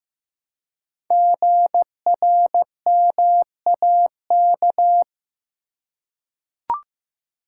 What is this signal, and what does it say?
Morse code, heard as a keyed tone, about 15 wpm, 700 Hz. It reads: GRMAK
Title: Code Groups - 5 Letters - 15wpm